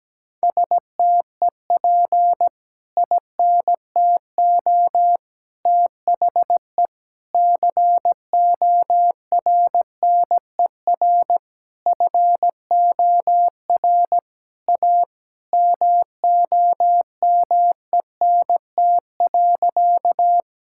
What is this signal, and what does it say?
Morse code, heard as a keyed tone, about 17 wpm, 700 Hz. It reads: STEP INTO THE CORNER FOR A MOMENT.
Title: Sentences from Top 1600 Words 17wpm